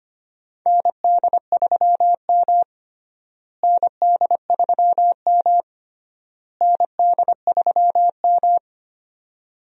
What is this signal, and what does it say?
Morse code, heard as a keyed tone, about 25 wpm, 700 Hz. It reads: ND3M ND3M ND3M
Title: USA Call Signs - 4 Characters - Repeated 3x - 25wpm